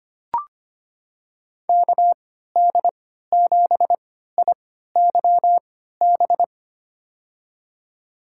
Morse code, heard as a keyed tone, about 25 wpm, spaced as for 14 wpm, 700 Hz. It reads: KD7IYB
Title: USA Call Signs - 6 Characters - 3x Character Spacing - 25wpm